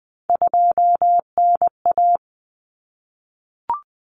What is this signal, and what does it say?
Morse code, heard as a keyed tone, about 20 wpm, 700 Hz. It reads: 2NA
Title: Common Cut Numbers - 3-Digit 20wpm